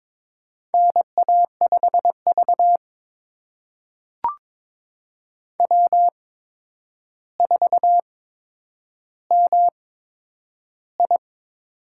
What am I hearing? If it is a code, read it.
Morse code, heard as a keyed tone, about 22 wpm, spaced as for 9 wpm, 700 Hz. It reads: NA5V W4MI
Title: USA Call Signs - 4 Characters - 8x Character Spacing - 22wpm